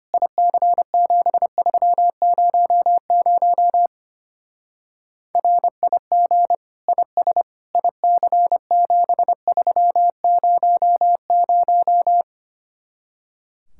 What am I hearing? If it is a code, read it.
Morse code, heard as a keyed tone, about 29 wpm, 700 Hz. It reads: IC7300 RIG IS IC7300
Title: Speed-Racing Common QSO Phrases Encore - 25wpm